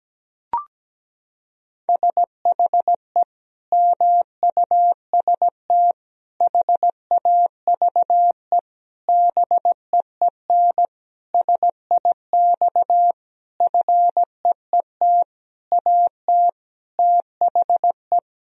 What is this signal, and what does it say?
Morse code, heard as a keyed tone, about 17 wpm, 700 Hz. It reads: SHE MUST HAVE BEEN SIX FEET AT THE
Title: Sentences from Top 1000 Words - Review Part 2 - 17wpm